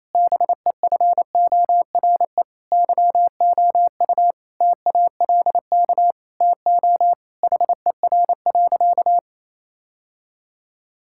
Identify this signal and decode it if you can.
Morse code, heard as a keyed tone, about 28 wpm, 700 Hz. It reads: BEFORE YOU TALK TO HER.
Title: Sentences from Top 1000 Words - Review Part 6 - 28wpm